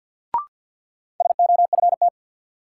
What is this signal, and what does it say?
Morse code, heard as a keyed tone, about 53 wpm, 700 Hz. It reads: SOFT
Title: Sentences from Top 1500 Words - New Word Drill - 50wpm